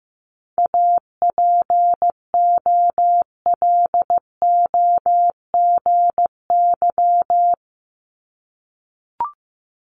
Morse code, heard as a keyed tone, about 15 wpm, 700 Hz. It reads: APOLOGY
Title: Multi-letter Phonemes 15wpm